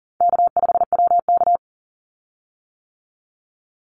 Morse code, heard as a keyed tone, about 40 wpm, 700 Hz. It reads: K5WK
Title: Easy Call Signs - 40wpm